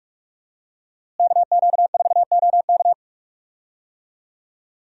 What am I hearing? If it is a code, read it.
Morse code, heard as a keyed tone, about 46 wpm, 700 Hz. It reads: KQ4OK
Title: USA Call Signs - 5 Characters - Repeated 3x - 45wpm